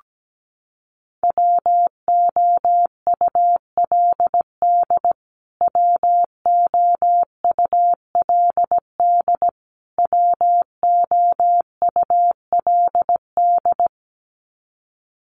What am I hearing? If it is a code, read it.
Morse code, heard as a keyed tone, about 17 wpm, 700 Hz. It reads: WOULD WOULD WOULD
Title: Top 500 Words Repeated 17wpm